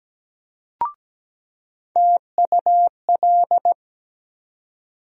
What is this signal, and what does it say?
Morse code, heard as a keyed tone, about 17 wpm, 700 Hz. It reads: TUL
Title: USA Call Sign Postfixes - 3 Letters 17wpm